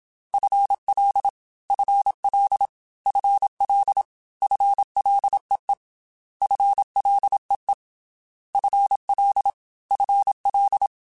Morse code, heard as a keyed tone, about 27 wpm, 800 Hz. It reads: FL FL FL FLEE FLEE FL FL
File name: FL / FL mp3s / FL Prefixes 3x2 Twice @25